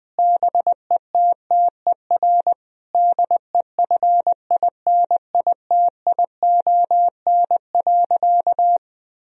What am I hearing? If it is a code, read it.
Morse code, heard as a keyed tone, about 20 wpm, 700 Hz. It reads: BETTER DEFINITION.
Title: Sentences from Top 1000 Words - Review Part 4 - 20wpm